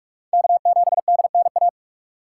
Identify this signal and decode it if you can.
Morse code, heard as a keyed tone, about 46 wpm, 700 Hz. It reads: K7DNA
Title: USA Call Signs - 5 Characters - 6x Character Spacing - 45wpm